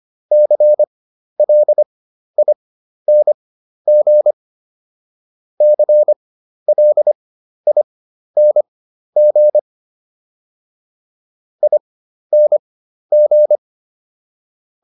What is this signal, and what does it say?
Morse code, heard as a keyed tone, about 25 wpm, spaced as for 12 wpm, 600 Hz. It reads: CLING CLING ING